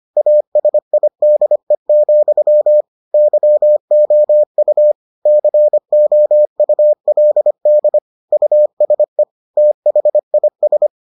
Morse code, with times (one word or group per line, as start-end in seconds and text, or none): 0.2-2.8 ASIDE,
3.1-4.9 YOU
5.3-8.0 COULD
8.3-9.2 USE
9.6-10.9 THIS